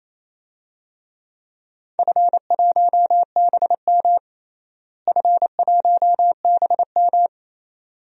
Morse code, heard as a keyed tone, about 28 wpm, 700 Hz. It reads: F1BM F1BM
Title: DXCC-100 Call Signs - 4 Characters - Repeated 2x - 28wpm